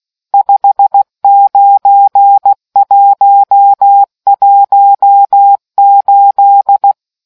Morse code, heard as a keyed tone, about 16 wpm, 800 Hz. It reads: 59118